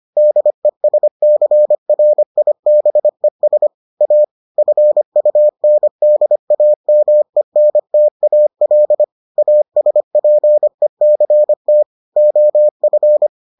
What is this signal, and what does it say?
Morse code, heard as a keyed tone, about 25 wpm, 600 Hz. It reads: DESCRIBES A FUNDAMENTAL ASPECT OF